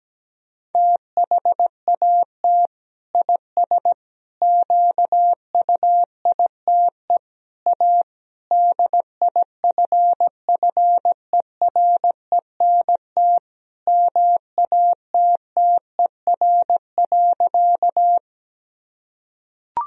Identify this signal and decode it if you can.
Morse code, heard as a keyed tone, about 17 wpm, 700 Hz. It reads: THAT IS QUITE A DIFFERENT MATTER.
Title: Sentences from Top 1000 Words - Review Part 3 - 17wpm